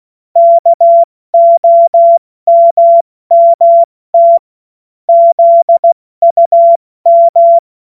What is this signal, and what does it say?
Morse code, heard as a keyed tone, about 16 wpm, 680 Hz. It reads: KOMMT ZUM
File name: Morsekurs / Witze / Witz 19. April 2022 70 F80